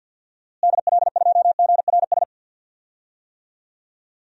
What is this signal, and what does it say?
Morse code, heard as a keyed tone, about 47 wpm, 700 Hz. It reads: DL2GRS
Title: DXCC-100 Call Signs - 6 Characters - Repeated 3x - 50wpm